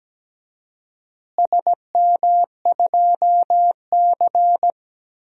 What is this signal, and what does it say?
Morse code, heard as a keyed tone, about 17 wpm, 700 Hz. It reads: SM2C
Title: DXCC-100 Call Signs - 4 Characters - Repeated 3x - 17wpm